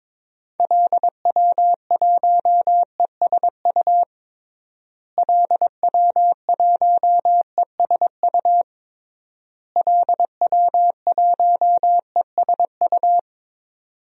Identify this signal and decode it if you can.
Morse code, heard as a keyed tone, about 22 wpm, 700 Hz. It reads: LW1ESU LW1ESU LW1ESU
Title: DXCC-100 Call Signs - 6 Characters - Repeated 3x - 22wpm